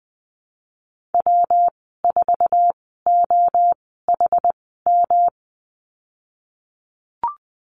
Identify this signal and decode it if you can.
Morse code, heard as a keyed tone, about 20 wpm, spaced as for 15 wpm, 700 Hz. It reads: W4OHM
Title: 122 Lesson 20g - Number 3 - Call Signs 20wpm